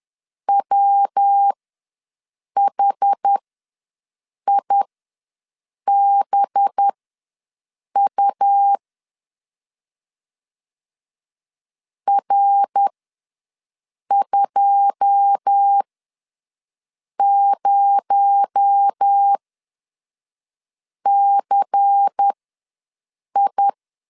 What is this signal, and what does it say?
Morse code, heard as a keyed tone, about 10 wpm, spaced as for 5 wpm, 800 Hz. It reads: WHIBU R20CI
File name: LEZIONI / 9 2 ws 10wpm 4130